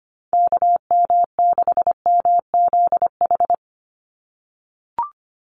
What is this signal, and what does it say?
Morse code, heard as a keyed tone, about 25 wpm, 700 Hz. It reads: KM6MZH
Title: USA Call Signs 25wpm